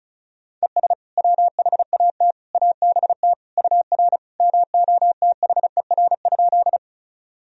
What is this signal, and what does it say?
Morse code, heard as a keyed tone, about 34 wpm, 700 Hz. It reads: ES WHAT ABT UR MOTHER?
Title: Sentences from Top 200 Words and Abbreviations 35wpm